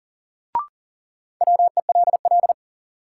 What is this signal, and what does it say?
Morse code, heard as a keyed tone, about 40 wpm, 700 Hz. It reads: WELL
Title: Sets of 2 Words - New Word Drill - 40wpm